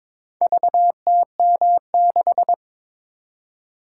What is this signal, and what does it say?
Morse code, heard as a keyed tone, about 22 wpm, 700 Hz. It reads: VTM6
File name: practice / 22wpm / Four Letter-Number Combinations - 22wpm